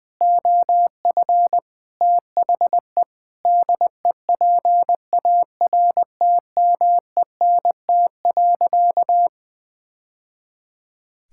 Morse code, heard as a keyed tone, about 20 wpm, 700 Hz. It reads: OF THE DEPARTMENT.
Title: Sentences with 3-Word Phrases - Spoken Beginning - 20wpm